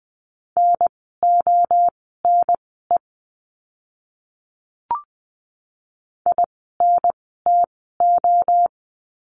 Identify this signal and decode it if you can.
Morse code, heard as a keyed tone, about 20 wpm, spaced as for 15 wpm, 700 Hz. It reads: NONE INTO
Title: 030 Lesson 6f - Letter i - Words 20wpm